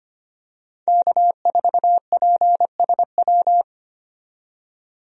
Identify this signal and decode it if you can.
Morse code, heard as a keyed tone, about 25 wpm, 690 Hz. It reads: K4PSW
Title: USA Call Signs - 5 Characters - 25wpm